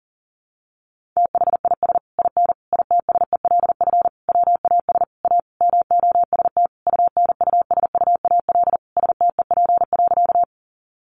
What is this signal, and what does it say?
Morse code, heard as a keyed tone, about 37 wpm, 700 Hz. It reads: THIS IN ITSELF WAS A MOST UNUSUAL STEP.
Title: Sentences from Top 1400 Words 40wpm